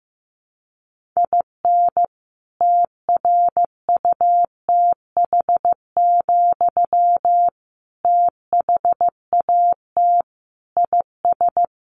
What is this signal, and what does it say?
Morse code, heard as a keyed tone, about 15 wpm, 700 Hz. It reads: IN TRUTH, THAT IS
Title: Sentences from Top 1300 Words 15wpm